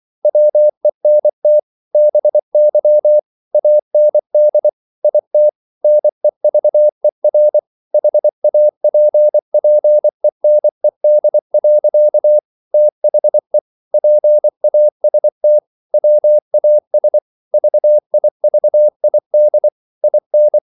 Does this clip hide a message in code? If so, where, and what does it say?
Yes: Morse code from 0.2 to 20.6 s, heard as a keyed tone, about 24 wpm, 600 Hz. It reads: WENT BY AND IT NEVER HAPPENED. THE PAST WAS VIVID IN